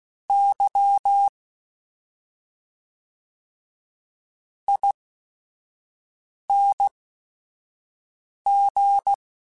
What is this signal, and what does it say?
Morse code, heard as a keyed tone, about 16 wpm, spaced as for 5 wpm, 800 Hz. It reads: Y ING
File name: Special INGs / Special INGs-mp3s / SlowToFaster ing @5